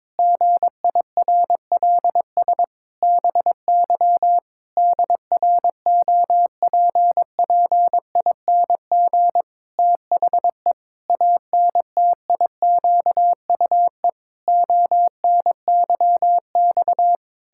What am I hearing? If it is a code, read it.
Morse code, heard as a keyed tone, about 22 wpm, 700 Hz. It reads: GIRLS BY DROPPING THE ANTIQUE ONYX
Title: Pangram Challenge v2 - 22wpm